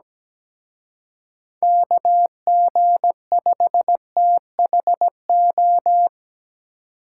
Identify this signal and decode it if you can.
Morse code, heard as a keyed tone, about 20 wpm, 700 Hz. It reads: KG5THO
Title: Speed Racing - USA 2x3 Call Signs 17wpm